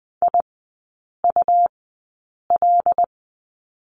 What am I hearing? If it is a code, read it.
Morse code, heard as a keyed tone, about 20 wpm, spaced as for 8 wpm, 690 Hz. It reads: IUL
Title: 191 Lesson 31e - Letter v - Three Characters 20wpm